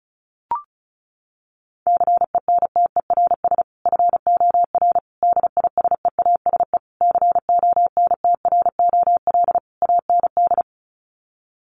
Morse code, heard as a keyed tone, about 34 wpm, 700 Hz. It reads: CENTERS FOR DISEASE CONTROL AND
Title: Sets of 5 Words Encore2 35wpm